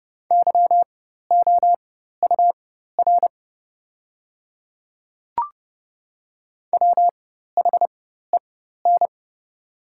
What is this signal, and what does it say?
Morse code, heard as a keyed tone, about 30 wpm, spaced as for 14 wpm, 700 Hz. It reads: YOUR WHEN
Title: 4x Character Spacing - Top 200 Words - 30wpm